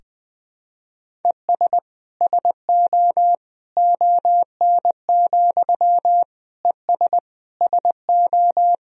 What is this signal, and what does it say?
Morse code, heard as a keyed tone, about 20 wpm, 700 Hz. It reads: ES SO ON, ES SO
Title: Sentences from Top 100 Words and Abbreviations 20wpm